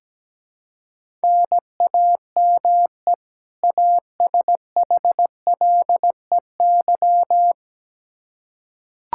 Morse code, heard as a keyed tone, about 17 wpm, 700 Hz. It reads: NAME ASHLEY
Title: Practice QSO - Exchange 1 - Beginning - 17wpm